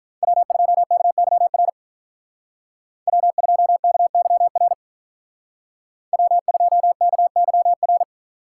Morse code, heard as a keyed tone, about 43 wpm, 700 Hz. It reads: W2KYR W2KYR W2KYR
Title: Speed Racing - USA 1x3 Call Signs 35wpm